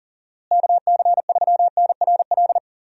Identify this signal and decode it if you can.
Morse code, heard as a keyed tone, about 40 wpm, 700 Hz. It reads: KC3NRL
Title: USA Call Signs - 6 Characters - Repeated 3x - 40wpm